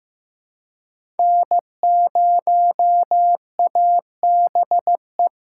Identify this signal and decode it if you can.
Morse code, heard as a keyed tone, about 16 wpm, 700 Hz. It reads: N0ABE
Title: USA Call Signs - 5 Characters - 2x Character Spacing - 15wpm